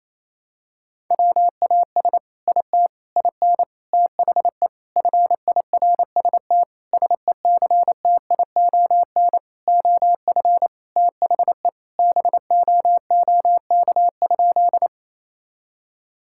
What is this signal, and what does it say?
Morse code, heard as a keyed tone, about 28 wpm, 700 Hz. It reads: WAS IT IN THE FIRST SECTION OF THE BOOK?
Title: Sentences from Top 1000 Words - Review Part 5 - 28wpm